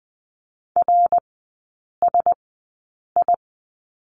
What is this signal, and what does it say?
Morse code, heard as a keyed tone, about 20 wpm, spaced as for 8 wpm, 700 Hz. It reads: RSI
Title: 127 Lesson 21e - Number 6 - Three Characters 20wpm